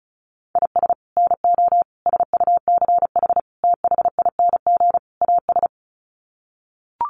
Morse code, heard as a keyed tone, about 34 wpm, 700 Hz. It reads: IS NO SUCH THING AS
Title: Sets of 5 Words Encore 35wpm